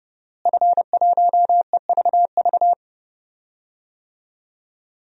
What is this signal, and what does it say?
Morse code, heard as a keyed tone, about 30 wpm, 700 Hz. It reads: F1EVV
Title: DXCC-100 Call Signs - 5 Characters - Repeated 2x - 30wpm